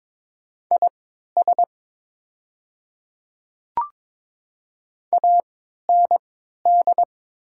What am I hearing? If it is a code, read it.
Morse code, heard as a keyed tone, about 22 wpm, spaced as for 12 wpm, 700 Hz. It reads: IS AND
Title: 3x Character Spacing - Top 100 Words - 22wpm